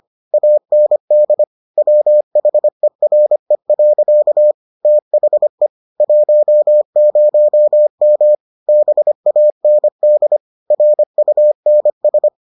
Morse code, heard as a keyed tone, about 25 wpm, 600 Hz. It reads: AND WHERE. THE 10M BAND RUNS